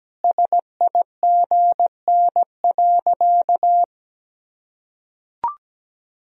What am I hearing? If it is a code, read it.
Morse code, heard as a keyed tone, about 17 wpm, 700 Hz. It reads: SIGN.
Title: Sentences from Top 1000 Words - Review Part 4 - 17wpm